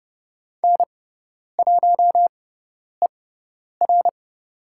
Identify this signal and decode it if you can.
Morse code, heard as a keyed tone, about 30 wpm, spaced as for 10 wpm, 700 Hz. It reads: N1ER
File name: Lesson 18g - Letter m - Call Signs 30wpm